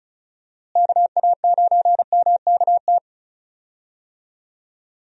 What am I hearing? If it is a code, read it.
Morse code, heard as a keyed tone, about 36 wpm, 700 Hz. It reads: KA9MKT
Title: USA Call Signs - 6 Characters - Repeated 2x - 35wpm